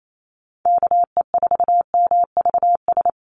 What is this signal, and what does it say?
Morse code, heard as a keyed tone, about 28 wpm, 700 Hz. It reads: KE4MVS
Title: USA Call Signs 28wpm